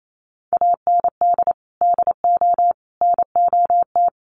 Morse code, heard as a keyed tone, about 28 wpm, 700 Hz. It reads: AND DO NOT
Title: Sentences from Top 1200 Words 28wpm